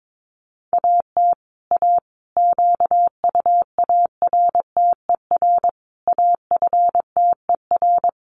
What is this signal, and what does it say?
Morse code, heard as a keyed tone, about 22 wpm, 700 Hz. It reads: AT A QUARTER AFTER
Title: Sentences from Top 1300 Words 22wpm